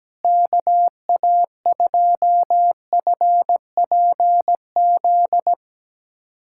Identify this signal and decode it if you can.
Morse code, heard as a keyed tone, about 17 wpm, 700 Hz. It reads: KA2FPZ
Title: USA Callsign DE Callsign 17wpm